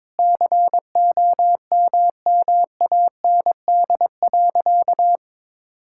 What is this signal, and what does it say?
Morse code, heard as a keyed tone, about 21 wpm, 700 Hz. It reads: COMMAND.
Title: Sentences from Top 1600 Words 22wpm